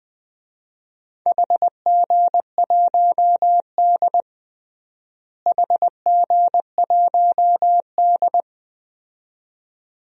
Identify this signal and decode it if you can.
Morse code, heard as a keyed tone, about 20 wpm, 700 Hz. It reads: HG1D HG1D
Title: DXCC-100 Call Signs - 4 Characters - Repeated 2x - 20wpm